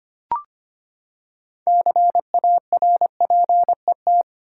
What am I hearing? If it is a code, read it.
Morse code, heard as a keyed tone, about 25 wpm, 700 Hz. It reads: CARPET
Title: Six Letter Words - 25wpm